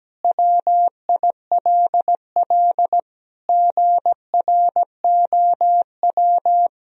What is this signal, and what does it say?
Morse code, heard as a keyed tone, about 17 wpm, 700 Hz. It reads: WILL GROW
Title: Sentences from Top 1000 Words - Review Part 2 - 17wpm